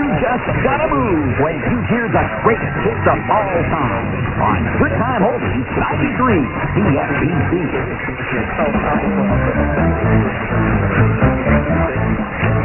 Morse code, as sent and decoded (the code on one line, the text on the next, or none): none